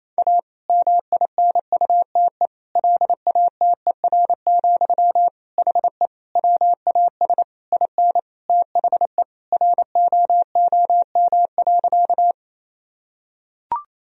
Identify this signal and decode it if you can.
Morse code, heard as a keyed tone, about 28 wpm, 700 Hz. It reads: A MINUTE LATER, HE WAS IN THE ROOM.
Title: Sentences from Top 1000 Words - Review Part 4 - 28wpm